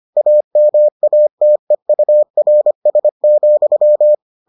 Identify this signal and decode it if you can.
Morse code, heard as a keyed tone, about 25 wpm, 600 Hz. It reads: AMATEURS,